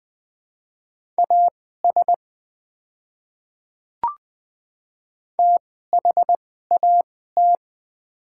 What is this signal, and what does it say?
Morse code, heard as a keyed tone, about 20 wpm, spaced as for 14 wpm, 700 Hz. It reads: AS THAT
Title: 2x Character Spacing - Top 100 Words - 20wpm